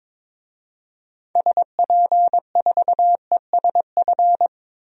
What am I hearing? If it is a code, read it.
Morse code, heard as a keyed tone, about 22 wpm, 700 Hz. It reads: SP4ESF
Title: DXCC-100 Call Signs - 6 Characters - Repeated 3x - 22wpm